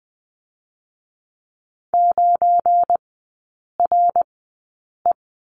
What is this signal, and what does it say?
Morse code, heard as a keyed tone, about 20 wpm, spaced as for 8 wpm, 700 Hz. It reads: 9RE
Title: 191 Lesson 31e - Letter v - Three Characters 20wpm